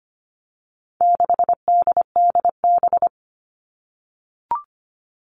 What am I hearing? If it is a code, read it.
Morse code, heard as a keyed tone, about 25 wpm, 700 Hz. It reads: 6DDB
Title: All Cut Numbers - 4-Digit 25wpm